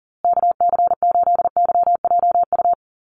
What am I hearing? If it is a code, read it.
Morse code, heard as a keyed tone, about 40 wpm, 700 Hz. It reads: KC8YJU
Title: USA Call Signs Repeated 3x 40wpm